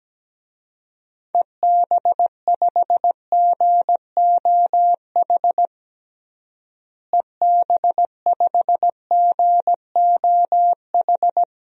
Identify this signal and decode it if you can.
Morse code, heard as a keyed tone, about 17 wpm, 700 Hz. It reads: EB5GOH EB5GOH
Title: DXCC-100 Call Signs - 6 Characters - Repeated 2x - 17wpm